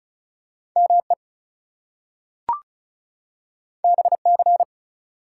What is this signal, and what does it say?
Morse code, heard as a keyed tone, about 36 wpm, 700 Hz. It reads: ME BC
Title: US States and Canadian Provinces 35wpm